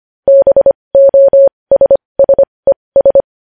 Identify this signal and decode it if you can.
Morse code, heard as a keyed tone, about 25 wpm, spaced as for 20 wpm, 550 Hz. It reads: BOSSES
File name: podcast / News headlines-1x-25x20